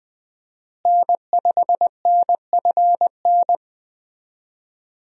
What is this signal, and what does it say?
Morse code, heard as a keyed tone, about 21 wpm, 700 Hz. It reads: N5NFN
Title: USA Call Signs - 5 Characters - Repeated 3x - 20wpm